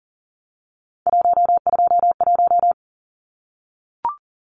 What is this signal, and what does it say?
Morse code, heard as a keyed tone, about 40 wpm, 700 Hz. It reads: 121
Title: Numbers - 3-Digit 40wpm